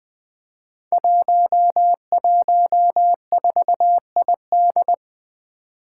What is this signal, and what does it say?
Morse code, heard as a keyed tone, about 20 wpm, 700 Hz. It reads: 114ID
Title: Five Challenging Letter-Number Combinations 20wpm